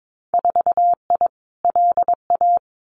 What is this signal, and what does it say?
Morse code, heard as a keyed tone, about 22 wpm, 700 Hz. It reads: 4I LA
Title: Winter Field Day Exchanges Encore 22wpm